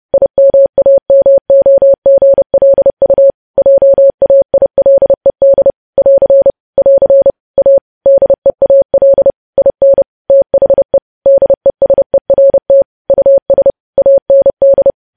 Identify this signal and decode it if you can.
Morse code, heard as a keyed tone, about 30 wpm, 550 Hz. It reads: IMAMOGLU JAILED <AR> <AR> A DEAL IN THE DESERT US AND
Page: news headlines in morse code at 30 wpm